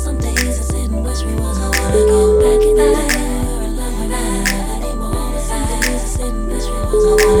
Ringback tone (Europe, 425 Hz), with noise.